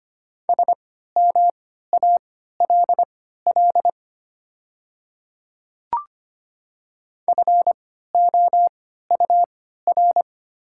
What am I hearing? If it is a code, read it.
Morse code, heard as a keyed tone, about 25 wpm, spaced as for 14 wpm, 700 Hz. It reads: SMALL FOUR
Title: 3x Character Spacing - Top 300 Words - 25wpm